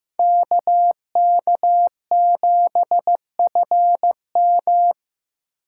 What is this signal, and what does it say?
Morse code, heard as a keyed tone, about 15 wpm, 700 Hz. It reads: KK7FM
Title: USA Call Signs - 8x Character Spacing - 15wpm